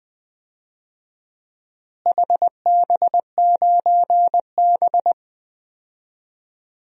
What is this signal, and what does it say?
Morse code, heard as a keyed tone, about 20 wpm, 700 Hz. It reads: HB9B
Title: DXCC-100 Call Signs - 4 Characters - Repeated 2x - 20wpm